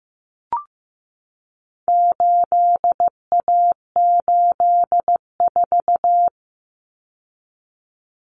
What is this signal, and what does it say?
Morse code, heard as a keyed tone, about 15 wpm, 700 Hz. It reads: 8A84
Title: Common Cut Numbers - 4-Digit 15wpm